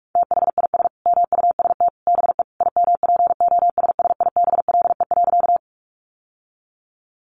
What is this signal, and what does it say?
Morse code, heard as a keyed tone, about 44 wpm, 700 Hz. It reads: THIS MUST BE IMPOSSIBLE.
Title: Sentences from Top 1100 Words 45wpm